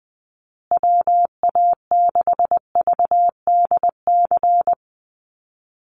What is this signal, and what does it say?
Morse code, heard as a keyed tone, about 20 wpm, 700 Hz. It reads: WA6VDC
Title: USA Call Signs Repeated 2x 20wpm